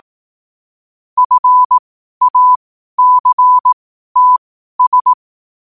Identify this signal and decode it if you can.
Morse code, heard as a keyed tone, about 18 wpm, spaced as for 12 wpm, 1000 Hz. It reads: FACTS
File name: qotdfiles / 12wpm